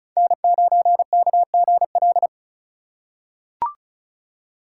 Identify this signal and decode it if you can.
Morse code, heard as a keyed tone, about 34 wpm, 700 Hz. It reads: N9KGL
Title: USA Call Signs - 5 Characters - 4x Character Spacing - 35wpm